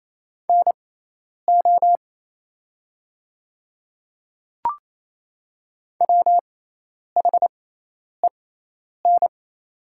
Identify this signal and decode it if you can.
Morse code, heard as a keyed tone, about 28 wpm, spaced as for 10 wpm, 700 Hz. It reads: NO WHEN
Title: 6x Character Spacing - Top 100 Words - 28wpm